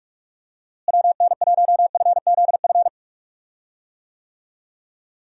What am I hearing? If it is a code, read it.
Morse code, heard as a keyed tone, about 46 wpm, 700 Hz. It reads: WN1FZF
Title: USA Call Signs - 6 Characters - Repeated 2x - 45wpm